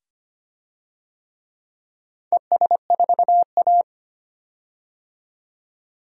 Morse code, heard as a keyed tone, about 25 wpm, 700 Hz. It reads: ES4A
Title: DXCC-100 Call Signs - 4 Characters - 25wpm